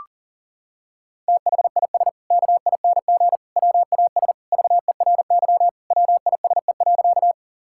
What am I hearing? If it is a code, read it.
Morse code, heard as a keyed tone, about 37 wpm, 700 Hz. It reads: THIS KING WAS VERY WISE.
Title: Sentences from Top 1600 Words 40wpm